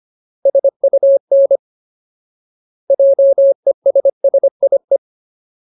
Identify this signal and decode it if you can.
Morse code, heard as a keyed tone, about 25 wpm, 550 Hz. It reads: SUN JESSIE
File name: Ultimate Word List / 25WPM Ultimate